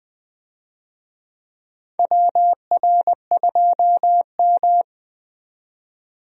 Morse code, heard as a keyed tone, about 20 wpm, 700 Hz. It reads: WR2M
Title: USA Call Signs - 4 Characters - 20wpm